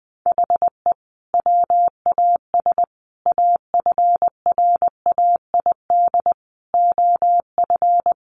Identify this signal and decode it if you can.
Morse code, heard as a keyed tone, about 20 wpm, 700 Hz. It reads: HE WAS AFRAID OF
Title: Sentences from Top 1500 Words 20wpm